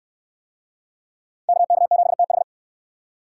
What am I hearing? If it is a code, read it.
Morse code, heard as a keyed tone, about 69 wpm, 700 Hz. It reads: KK7TH